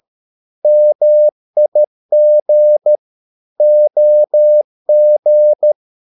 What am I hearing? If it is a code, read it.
Morse code, heard as a keyed tone, about 13 wpm, 600 Hz. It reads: MIG OG
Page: Ferie 13 wpm